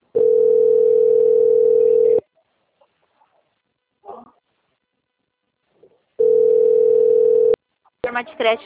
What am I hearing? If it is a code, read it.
Ringback tone (North America), 440 + 480 Hz, with speech.
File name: out- - - - - .34519